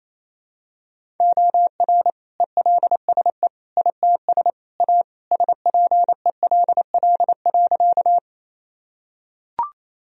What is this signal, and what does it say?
Morse code, heard as a keyed tone, about 28 wpm, 700 Hz. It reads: OR ELSE ITS A SPELL.
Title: Sentences from Top 1000 Words - Review Part 1 - 28wpm